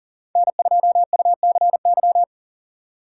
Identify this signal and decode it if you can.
Morse code, heard as a keyed tone, about 40 wpm, 700 Hz. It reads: N2UCY
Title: USA Call Signs - 5 Characters - 6x Character Spacing - 40wpm